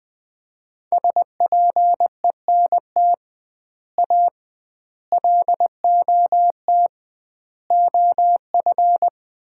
Morse code, heard as a keyed tone, about 20 wpm, 700 Hz. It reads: SPENT A LOT OF
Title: 2x Word-Spacing - Sets of 5 Words - 20wpm